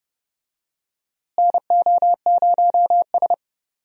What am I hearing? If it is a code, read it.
Morse code, heard as a keyed tone, about 31 wpm, 700 Hz. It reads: NO0S
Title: USA Call Signs - 4 Characters - Repeated 3x - 30wpm